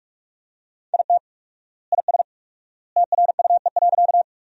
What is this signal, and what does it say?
Morse code, heard as a keyed tone, about 44 wpm, 700 Hz. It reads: IT IS TRUE.
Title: 4x Word-Spacing - Sentences from Top 400 Words - 45wpm